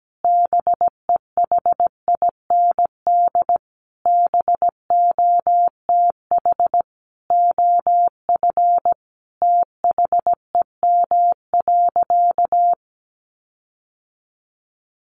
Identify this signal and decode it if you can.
Morse code, heard as a keyed tone, about 17 wpm, 700 Hz. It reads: BEHIND BOTH OF THEM.
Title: Sentences from Top 1400 Words 17wpm